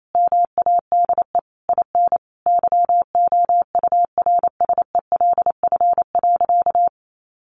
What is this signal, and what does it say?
Morse code, heard as a keyed tone, about 28 wpm, 700 Hz. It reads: MADE IN YOURSELF.
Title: Sentences from Top 600 Words 28wpm